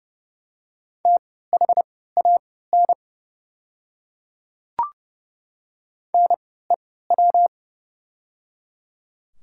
Morse code, heard as a keyed tone, about 31 wpm, spaced as for 17 wpm, 700 Hz. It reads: THAN NEW
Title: 3x Character Spacing - Top 300 Words - 30wpm